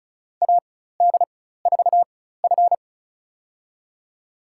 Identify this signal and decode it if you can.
Morse code, heard as a keyed tone, about 36 wpm, spaced as for 16 wpm, 700 Hz. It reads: AD4F
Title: USA Call Signs - 4 Characters - 4x Character Spacing - 35wpm